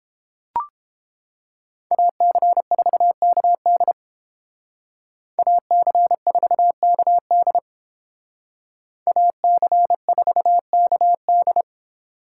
Morse code, heard as a keyed tone, about 29 wpm, 700 Hz. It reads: AC4KD AC4KD AC4KD